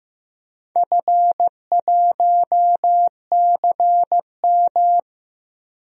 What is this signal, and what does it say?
Morse code, heard as a keyed tone, about 15 wpm, 700 Hz. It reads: F1CM
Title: DXCC-100 Call Signs - 4 Characters - Repeated 2x - 15wpm